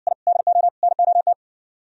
Morse code, heard as a keyed tone, about 59 wpm, 700 Hz. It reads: I DO NOT